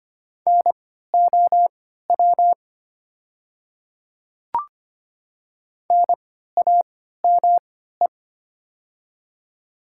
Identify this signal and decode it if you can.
Morse code, heard as a keyed tone, about 25 wpm, spaced as for 14 wpm, 690 Hz. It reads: NOW NAME
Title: 3x Character Spacing - Top 200 Words - 25wpm